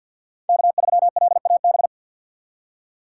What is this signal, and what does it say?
Morse code, heard as a keyed tone, about 52 wpm, 700 Hz. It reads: K3LAB
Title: USA Call Signs - 5 Characters - 3x Character Spacing - 50wpm